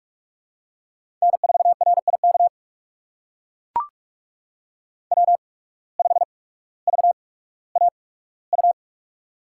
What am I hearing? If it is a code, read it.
Morse code, heard as a keyed tone, about 44 wpm, spaced as for 19 wpm, 700 Hz. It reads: N4RIK W5VAU
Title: USA Call Signs - 5 Characters - 8x Character Spacing - 45wpm